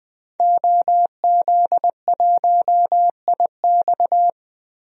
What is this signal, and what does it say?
Morse code, heard as a keyed tone, about 20 wpm, 700 Hz. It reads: OZ1IX
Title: DXCC-100 Call Signs - 5 Characters - 20wpm